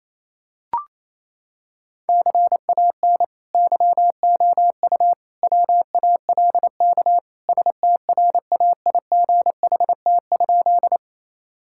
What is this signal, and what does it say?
Morse code, heard as a keyed tone, about 28 wpm, 700 Hz. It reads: CAN YOU WALK STRAIGHT?
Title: Sentences from Top 1000 Words - Review Part 6 - 28wpm